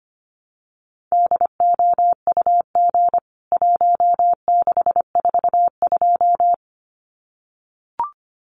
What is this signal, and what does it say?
Morse code, heard as a keyed tone, about 25 wpm, 700 Hz. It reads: DOUG 1642
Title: CWT Exchange 25wpm